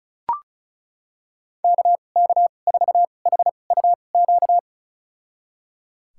Morse code, heard as a keyed tone, about 34 wpm, spaced as for 25 wpm, 700 Hz. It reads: KK4HUQ
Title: USA Call Signs - 6 Characters - 2x Character Spacing - 35wpm